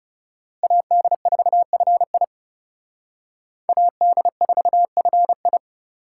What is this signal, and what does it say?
Morse code, heard as a keyed tone, about 38 wpm, 700 Hz. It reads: AD4FI AD4FI